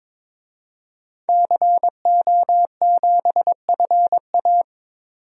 Morse code, heard as a keyed tone, about 22 wpm, 700 Hz. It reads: CO7FA
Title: DXCC-100 Call Signs - 5 Characters - 22wpm